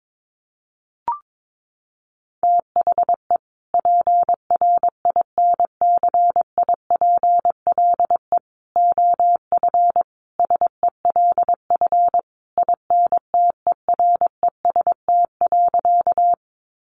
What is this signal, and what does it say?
Morse code, heard as a keyed tone, about 22 wpm, 700 Hz. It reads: THE PRINCIPLE OF SELF INTEREST.
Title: Sentences from Top 1400 Words 22wpm